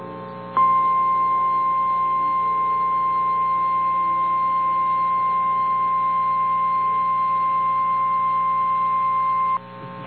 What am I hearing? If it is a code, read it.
Emergency Alert System (SAME) tones (the 1050 Hz attention tone). It is heard from 0.5 to 9.6 s.
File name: eas / KNCYKBIE / KNCYKBIE